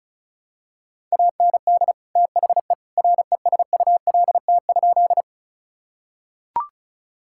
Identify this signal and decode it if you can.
Morse code, heard as a keyed tone, about 34 wpm, 700 Hz. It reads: AND THE RESULT?
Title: Sentences from Top 1000 Words - Review Part 5 - 35wpm